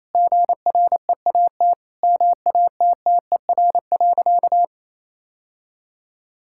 Morse code, heard as a keyed tone, about 28 wpm, 700 Hz. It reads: GREAT MATTER.
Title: Sentences from Top 1600 Words 28wpm